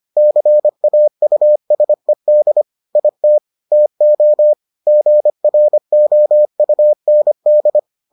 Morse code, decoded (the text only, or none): CAUSED IT TO GROUND